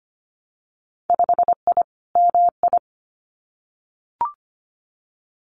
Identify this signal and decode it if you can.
Morse code, heard as a keyed tone, about 25 wpm, 700 Hz. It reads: 5I MI
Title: Winter Field Day Exchanges Encore 25wpm